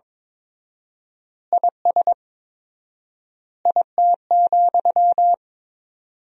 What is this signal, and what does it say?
Morse code, heard as a keyed tone, about 22 wpm, 700 Hz. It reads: IS IT,
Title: 4x Word-Spacing - Sentences from Top 300 Words - 22wpm